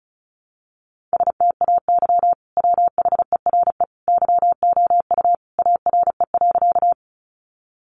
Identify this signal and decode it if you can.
Morse code, heard as a keyed tone, about 34 wpm, 700 Hz. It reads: STAY WHERE YOU ARE.
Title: Sentences from Top 500 Words 35wpm